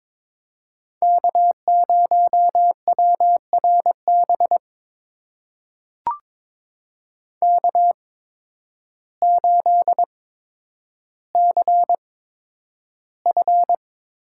Morse code, heard as a keyed tone, about 22 wpm, spaced as for 10 wpm, 700 Hz. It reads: K0WRB K8CF
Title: USA Call Signs - 5 Characters - 8x Character Spacing - 22wpm